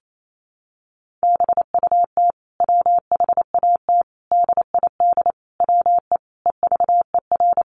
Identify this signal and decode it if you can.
Morse code, heard as a keyed tone, about 28 wpm, 700 Hz. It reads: BUT WHAT DID WE EVER
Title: Sentences from Top 400 Words 28wpm